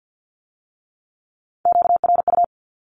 Morse code, heard as a keyed tone, about 50 wpm, 700 Hz. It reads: QRU
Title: Q-Codes with Spoken Meaning - 50wpm